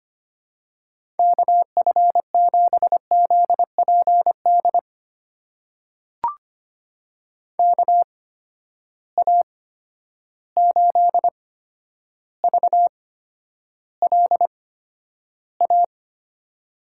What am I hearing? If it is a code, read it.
Morse code, heard as a keyed tone, about 25 wpm, spaced as for 11 wpm, 700 Hz. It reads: KF7ZPD KA8VLA
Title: USA Call Signs - 6 Characters - 8x Character Spacing - 25wpm